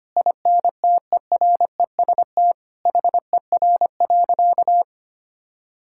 Morse code, heard as a keyed tone, about 25 wpm, 700 Hz. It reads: INTEREST HER.
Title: Sentences from Top 1000 Words - Review Part 5 - 25wpm